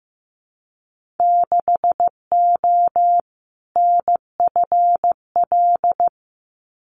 Morse code, heard as a keyed tone, about 15 wpm, 700 Hz. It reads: 6O NFL
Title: Winter Field Day Exchanges 15wpm